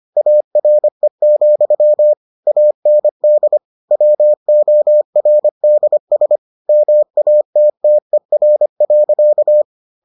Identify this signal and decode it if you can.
Morse code, heard as a keyed tone, about 25 wpm, 600 Hz. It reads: ARE, AND WORDS MATTER.